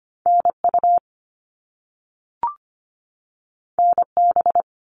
Morse code, heard as a keyed tone, about 25 wpm, 700 Hz. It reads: NU NB
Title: Canadian Provinces and Territories 25wpm